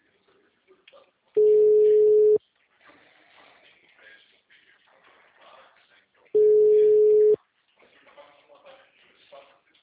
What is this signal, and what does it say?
Ringback tone (Europe, 425 Hz), with speech.
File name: out- -2017- - - .36